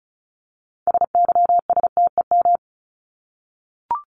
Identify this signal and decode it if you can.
Morse code, heard as a keyed tone, about 34 wpm, 700 Hz. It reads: SYSTEM
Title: Top 501-750 Words 35wpm